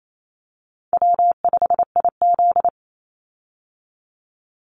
Morse code, heard as a keyed tone, about 28 wpm, 700 Hz. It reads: W5IZ
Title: Call Signs Repeated 3x - 28wpm